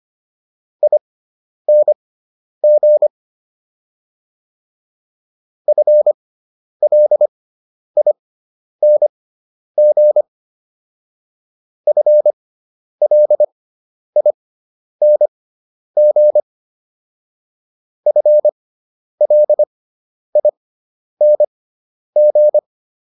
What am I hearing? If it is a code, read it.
Morse code, heard as a keyed tone, about 25 wpm, spaced as for 10 wpm, 600 Hz. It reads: ING FLING FLING FLING